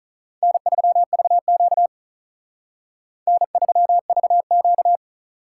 Morse code, heard as a keyed tone, about 43 wpm, 700 Hz. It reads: N3VQ N3VQ
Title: Speed Racing - USA 1x2 Call Signs 35wpm